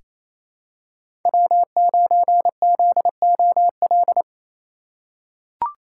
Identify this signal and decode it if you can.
Morse code, heard as a keyed tone, about 28 wpm, 700 Hz. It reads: W9ZOL
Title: USA Call Signs - 5 Characters - 2x Character Spacing - 28wpm